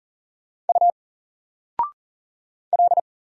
Morse code, heard as a keyed tone, about 40 wpm, 700 Hz. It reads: U L
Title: Single Letter-Number 40wpm